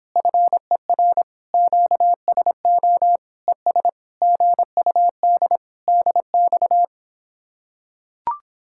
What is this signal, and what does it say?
Morse code, heard as a keyed tone, about 26 wpm, 700 Hz. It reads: FER QSO ES GUD DX